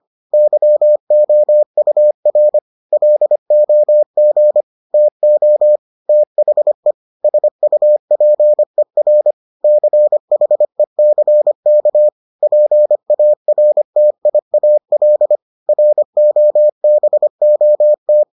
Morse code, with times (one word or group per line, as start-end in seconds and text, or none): 0.3-2.6 YOUR
2.9-4.6 LOG
4.9-5.8 TO
6.1-6.9 THE
7.2-9.3 SUPER
9.6-12.1 CHECK
12.4-15.4 PARTIAL
15.7-18.2 ROBOT